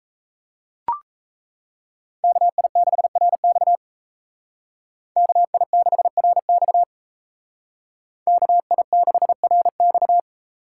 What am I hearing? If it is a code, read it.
Morse code, heard as a keyed tone, about 32 wpm, 700 Hz. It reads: KI6RX KI6RX KI6RX
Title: Speed Racing - USA 2x2 Call Signs 28wpm